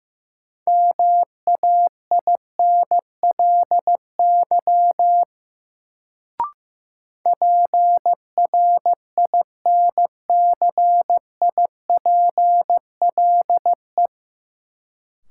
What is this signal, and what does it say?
Morse code, heard as a keyed tone, about 15 wpm, 700 Hz. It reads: MAINLY PRINCIPLE
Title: Sentences from Top 1400 Words - New Word Drill - 15wpm